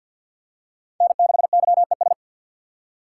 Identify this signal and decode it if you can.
Morse code, heard as a keyed tone, about 46 wpm, 700 Hz. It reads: N6YES
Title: USA Call Signs - 3x Character Spacing - 50wpm